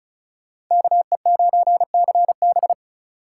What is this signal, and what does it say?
Morse code, heard as a keyed tone, about 38 wpm, 700 Hz. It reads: KE9CB